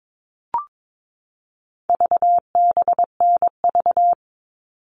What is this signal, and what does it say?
Morse code, heard as a keyed tone, about 22 wpm, 700 Hz. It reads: VBNV
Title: All Cut Numbers - 4-Digit 22wpm